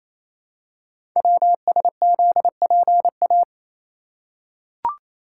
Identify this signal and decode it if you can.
Morse code, heard as a keyed tone, about 28 wpm, 700 Hz. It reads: WSZPA
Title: Code Groups - 5 Letters - 28wpm